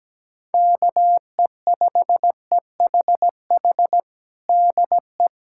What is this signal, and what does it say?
Morse code, heard as a keyed tone, about 17 wpm, 700 Hz. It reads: KE5EHH DE
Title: USA Callsign DE Callsign - 6 Characters - 17wpm